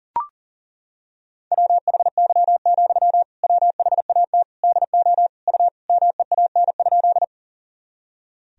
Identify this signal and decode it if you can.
Morse code, heard as a keyed tone, about 37 wpm, 700 Hz. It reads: WHY, WHAT DO U MEAN?
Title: Sentences from Top 200 Words and Abbreviations 40wpm